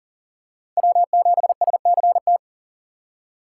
Morse code, heard as a keyed tone, about 38 wpm, 700 Hz. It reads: W7SCT
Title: USA Call Signs - 5 Characters - 8x Character Spacing - 40wpm